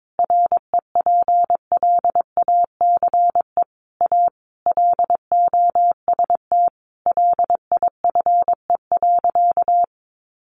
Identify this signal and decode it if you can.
Morse code, heard as a keyed tone, about 22 wpm, 700 Hz. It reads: REPLACE A LOST LIFE.
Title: Sentences from Top 1400 Words 22wpm